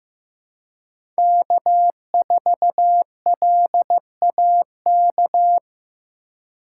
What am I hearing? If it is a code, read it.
Morse code, heard as a keyed tone, about 15 wpm, 700 Hz. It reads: K4LAK
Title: USA Call Signs - 5 Characters - 2x Character Spacing - 15wpm